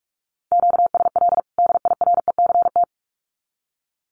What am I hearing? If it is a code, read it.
Morse code, heard as a keyed tone, about 45 wpm, 700 Hz. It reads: QSL DIRECT
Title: Common QSO Phrases 45wpm